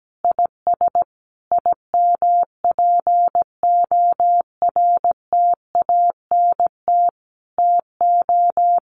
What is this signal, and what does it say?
Morse code, heard as a keyed tone, about 17 wpm, 700 Hz. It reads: IS IMPORTANT TO
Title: Sentences from Top 1400 Words 17wpm